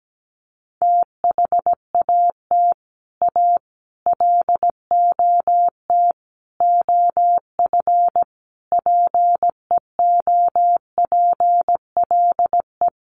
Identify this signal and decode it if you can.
Morse code, heard as a keyed tone, about 17 wpm, 700 Hz. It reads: THAT A LOT OF PEOPLE
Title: Sets of 5 Words 17wpm